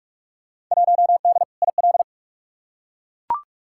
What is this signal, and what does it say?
Morse code, heard as a keyed tone, about 43 wpm, 700 Hz. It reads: 1D IL
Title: Field Day Exchanges - 45wpm